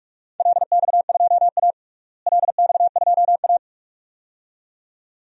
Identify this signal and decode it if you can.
Morse code, heard as a keyed tone, about 46 wpm, 700 Hz. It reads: LX2A LX2A
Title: DXCC-100 Call Signs - 4 Characters - Repeated 2x - 45wpm